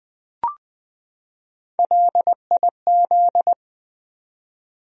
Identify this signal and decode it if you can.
Morse code, heard as a keyed tone, about 21 wpm, 700 Hz. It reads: LIZ
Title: USA Common Names - Max 3 Letters - 20wpm